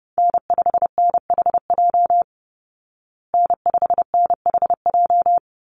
Morse code, heard as a keyed tone, about 30 wpm, 700 Hz. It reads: N5NHJ N5NHJ
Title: Call Signs Repeated 3x - 30wpm